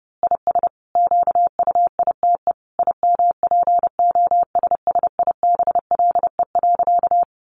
Morse code, heard as a keyed tone, about 30 wpm, 700 Hz. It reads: IS QUITE IMPOSSIBLE.
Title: Sentences from Top 1100 Words 30wpm